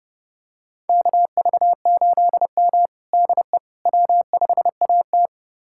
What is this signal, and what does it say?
Morse code, heard as a keyed tone, about 30 wpm, 700 Hz. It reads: KV8M DE W5AT
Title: USA Callsign DE Callsign - 4 Characters - 30wpm